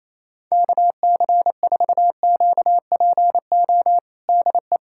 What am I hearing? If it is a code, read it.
Morse code, heard as a keyed tone, about 28 wpm, 700 Hz. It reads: KC4QPO DE
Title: USA Callsign DE Callsign 28wpm